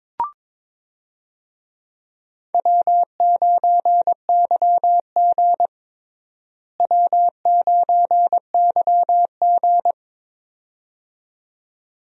Morse code, heard as a keyed tone, about 22 wpm, 700 Hz. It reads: W9YG W9YG
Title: USA Call Signs - 4 Characters - Repeated 2x - 22wpm